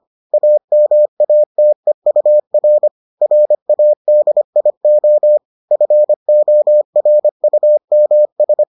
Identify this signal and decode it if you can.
Morse code, heard as a keyed tone, about 25 wpm, 600 Hz. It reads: AMATEUR RADIO FORUMS